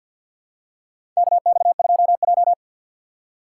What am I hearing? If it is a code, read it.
Morse code, heard as a keyed tone, about 50 wpm, 700 Hz. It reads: KX2J
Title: USA Call Signs - 6x Character Spacing - 50wpm